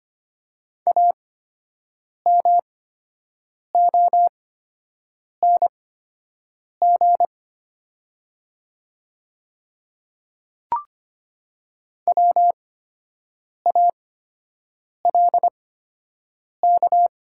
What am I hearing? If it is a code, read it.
Morse code, heard as a keyed tone, about 25 wpm, spaced as for 7 wpm, 700 Hz. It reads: AMONG WALK
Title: 8x Character Spacing - Top 500 Words - 25wpm